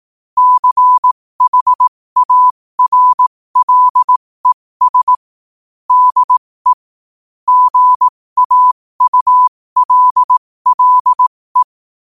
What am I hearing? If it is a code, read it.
Morse code, heard as a keyed tone, about 18 wpm, spaced as for 15 wpm, 1000 Hz. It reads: CHARLES DE GAULLE
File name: 15wpm